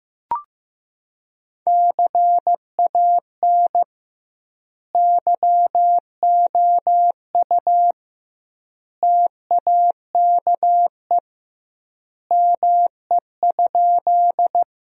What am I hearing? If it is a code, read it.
Morse code, heard as a keyed tone, about 15 wpm, 700 Hz. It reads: CAN YOU TAKE ME?
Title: 2x Word-Spacing - Sentences from Top 200 Words - 15wpm